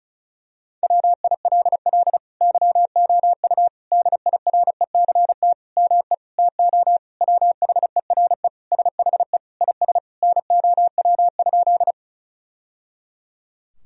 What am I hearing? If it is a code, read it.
Morse code, heard as a keyed tone, about 34 wpm, 700 Hz. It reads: WILL YOU DIRECT ME TO WHERE SHE IS NOW?
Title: Sentences from Top 1000 Words - Review Part 2 - 35wpm